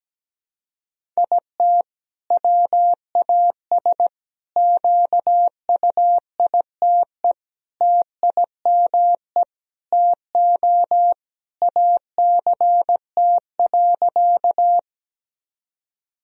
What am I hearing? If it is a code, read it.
Morse code, heard as a keyed tone, about 17 wpm, 700 Hz. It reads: IT WAS QUITE TIME TO ACT.
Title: Sentences from Top 1000 Words - Review Part 1 - 17wpm